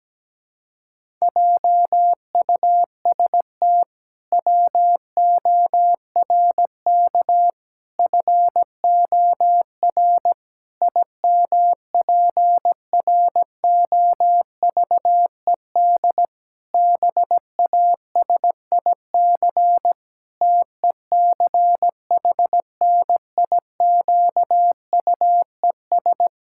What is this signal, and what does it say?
Morse code, heard as a keyed tone, about 17 wpm, 700 Hz. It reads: JUST WORK FOR IMPROVED BASIC TECHNIQUES
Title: Pangram Challenge v2 - 17wpm